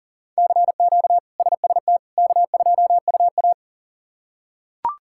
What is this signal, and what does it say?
Morse code, heard as a keyed tone, about 37 wpm, 700 Hz. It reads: CQ SST K2UA
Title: K1USN SST QSO Practice - 40wpm